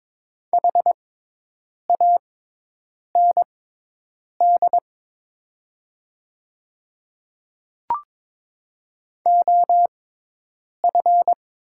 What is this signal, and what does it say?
Morse code, heard as a keyed tone, about 22 wpm, spaced as for 8 wpm, 700 Hz. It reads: HAND OF
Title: 6x Character Spacing - Top 200 Words - 22wpm